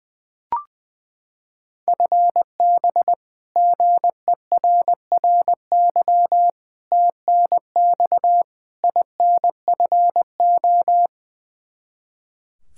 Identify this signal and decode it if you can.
Morse code, heard as a keyed tone, about 20 wpm, 700 Hz. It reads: FB GERRY TNX INFO
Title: Practice QSO - Exchange 3 - More Info - 20wpm